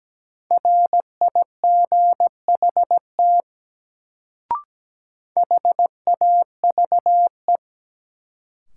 Morse code, heard as a keyed tone, about 17 wpm, 700 Hz. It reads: RIGHT HAVE
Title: Sets of 2 Words - New Word Drill - 17wpm